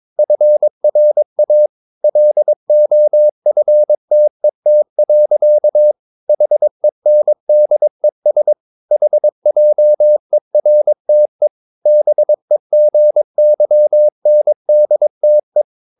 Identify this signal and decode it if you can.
Morse code, heard as a keyed tone, about 22 wpm, 600 Hz. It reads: FRA LOFTET. HENDES HJERTE BEGYNDTE
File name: Ferie QRS 22wpm